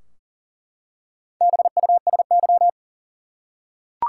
Morse code, heard as a keyed tone, about 40 wpm, 700 Hz. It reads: BUSY
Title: Four Letter Words - 40wpm